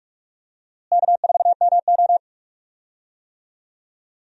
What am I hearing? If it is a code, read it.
Morse code, heard as a keyed tone, about 42 wpm, 700 Hz. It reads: K4MO
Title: USA Call Signs - 4 Characters - Repeated 3x - 45wpm